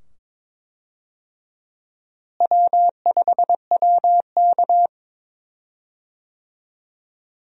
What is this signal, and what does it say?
Morse code, heard as a keyed tone, about 22 wpm, 700 Hz. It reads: W5WK
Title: USA Call Signs - 4 Characters - Repeated 2x - 22wpm